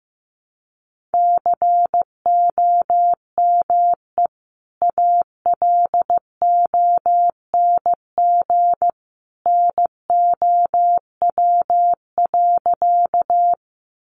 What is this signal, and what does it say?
Morse code, heard as a keyed tone, about 15 wpm, 700 Hz. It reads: COME ALONG NOW.
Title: Sentences from Top 700 Words 15wpm